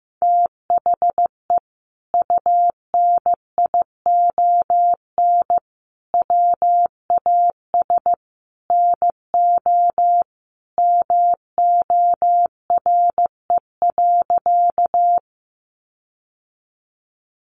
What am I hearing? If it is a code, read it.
Morse code, heard as a keyed tone, about 14 wpm, 700 Hz. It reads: THE UNION WAS NO MORE.
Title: Sentences from Top 1400 Words 15wpm